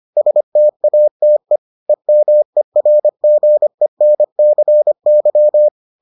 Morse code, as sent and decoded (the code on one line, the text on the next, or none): ... - .- - . / . -- . .-. --. -. -.-. -.--
STATE EMERGENCY